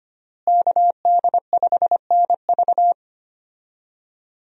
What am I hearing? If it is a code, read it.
Morse code, heard as a keyed tone, about 25 wpm, 700 Hz. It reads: KD5NV
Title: USA Call Signs - 5 Characters - Repeated 3x - 25wpm